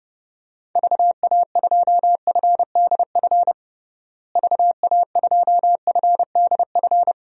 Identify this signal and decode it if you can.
Morse code, heard as a keyed tone, about 32 wpm, 700 Hz. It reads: VA2FDF VA2FDF
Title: DXCC-100 Call Signs - 6 Characters - Repeated 2x - 30wpm